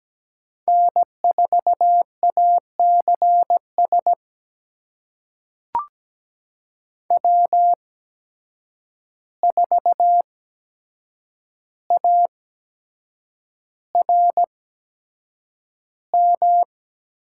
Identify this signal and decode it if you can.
Morse code, heard as a keyed tone, about 17 wpm, spaced as for 7 wpm, 700 Hz. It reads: N4ACS W4ARM
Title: USA Call Signs - 5 Characters - 8x Character Spacing - 17wpm